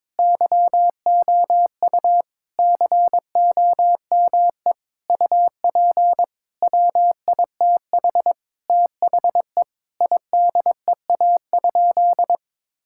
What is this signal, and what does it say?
Morse code, heard as a keyed tone, about 22 wpm, 700 Hz. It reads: YOU COME UP WITH THE IDEA?
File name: practice / 22wpm / Sentences with 8-Word Phrases - 22wpm